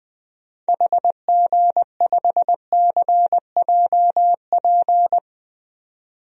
Morse code, heard as a keyed tone, about 20 wpm, 700 Hz. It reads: HG5CJP
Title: DXCC-100 Call Signs - 6 Characters - 20wpm